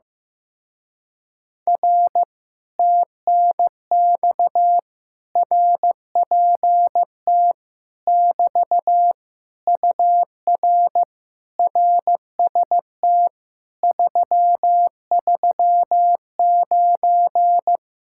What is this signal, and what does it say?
Morse code, heard as a keyed tone, about 15 wpm, 700 Hz. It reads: R TNX RPT <BT> UR RST 339
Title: Speed-Racing Common QSO Phrases Encore - 15wpm